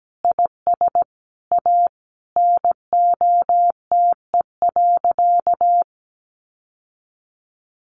Morse code, heard as a keyed tone, about 17 wpm, 700 Hz. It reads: IS A NOTE.
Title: Sentences from Top 700 Words 17wpm